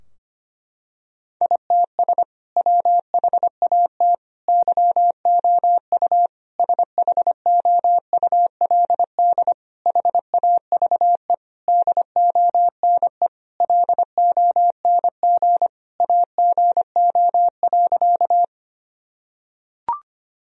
Morse code, heard as a keyed tone, about 25 wpm, 700 Hz. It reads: ITS WHAT YOU SHOULD HAVE DONE LONG AGO.
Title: Sentences from Top 1000 Words - Review Part 3 - 25wpm